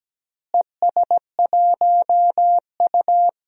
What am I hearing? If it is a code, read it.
Morse code, heard as a keyed tone, about 17 wpm, 700 Hz. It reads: ES1U
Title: DXCC-100 Call Signs - 4 Characters - Repeated 3x - 17wpm